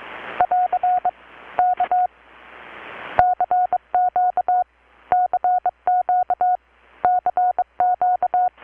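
Morse code, heard as a keyed tone, about 22 wpm, 720 Hz. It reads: <AR> K CQ CQ CQ
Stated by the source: Morse from KPH's Night of Nights XX maritime radiotelegraphy event, received in Chiapas, Mexico about 3,800 km away on a Kenwood R-600 with a grounded 10 m long-wire antenna.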